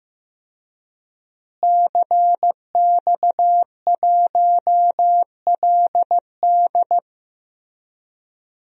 Morse code, heard as a keyed tone, about 16 wpm, 700 Hz. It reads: CX1LD
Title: DXCC-100 Call Signs - 5 Characters - Repeated 3x - 15wpm